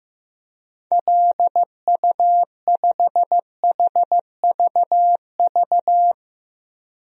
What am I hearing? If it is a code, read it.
Morse code, heard as a keyed tone, about 16 wpm, 700 Hz. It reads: LU5HVV
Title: DXCC-100 Call Signs - 6 Characters - 15wpm